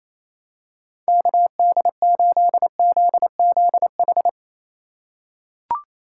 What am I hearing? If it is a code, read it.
Morse code, heard as a keyed tone, about 28 wpm, 700 Hz. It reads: KD8ZZH
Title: USA Call Signs - 6 Characters - 3x Character Spacing - 28wpm